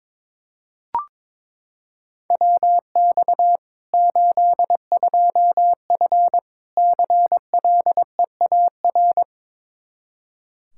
Morse code, heard as a keyed tone, about 22 wpm, 700 Hz. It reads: WX 82F CLEAR
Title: Practice QSO - Exchange 2 - Info - 22wpm